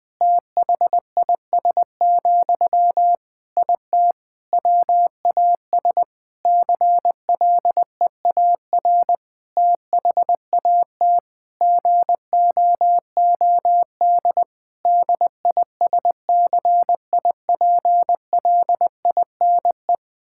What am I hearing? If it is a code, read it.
Morse code, heard as a keyed tone, about 20 wpm, 700 Hz. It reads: THIS, IT WAS CLEAR THAT GOOD DISCIPLINE
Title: Sentences from Top 1000 Words - Review Part 6 - 20wpm